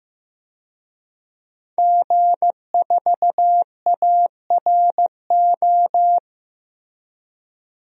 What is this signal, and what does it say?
Morse code, heard as a keyed tone, about 15 wpm, 700 Hz. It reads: G4ARO
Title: DXCC-100 Call Signs - 5 Characters - Repeated 2x - 15wpm